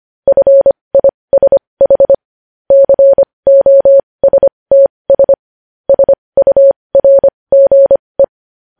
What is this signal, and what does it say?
Morse code, heard as a keyed tone, about 25 wpm, spaced as for 20 wpm, 550 Hz. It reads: FISH COSTS SURGE
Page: news headlines in morse code at 20 wpm